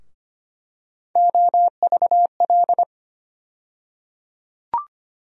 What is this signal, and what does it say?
Morse code, heard as a keyed tone, about 25 wpm, 700 Hz. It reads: OVL
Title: USA Call Sign Postfixes - 3 Letters 25wpm